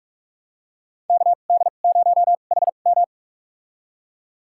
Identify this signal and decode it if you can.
Morse code, heard as a keyed tone, about 42 wpm, spaced as for 31 wpm, 690 Hz. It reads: KD0HM
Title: USA Call Signs - 5 Characters - 2x Character Spacing - 45wpm